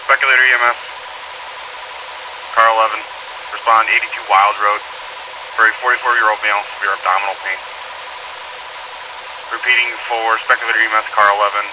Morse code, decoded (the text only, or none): none